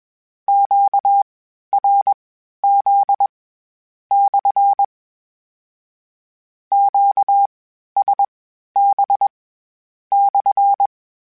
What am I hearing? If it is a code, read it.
Morse code, heard as a keyed tone, about 21 wpm, spaced as for 11 wpm, 800 Hz. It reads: QRZ/ QSB/
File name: Q Calls / Q Calls-mp3s / Q Calls SPSL1 @20